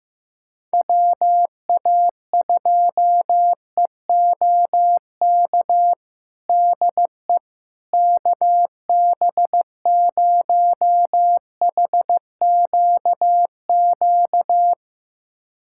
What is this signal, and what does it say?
Morse code, heard as a keyed tone, about 15 wpm, 700 Hz. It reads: WA2EOK DE KB0HQQ
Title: USA Callsign DE Callsign - 6 Characters - 15wpm